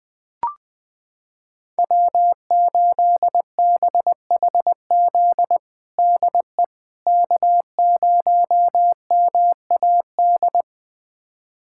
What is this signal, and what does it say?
Morse code, heard as a keyed tone, about 20 wpm, 700 Hz. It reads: W8BHZ DE K0MAD
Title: USA Callsign DE Callsign - 5 Characters - 20wpm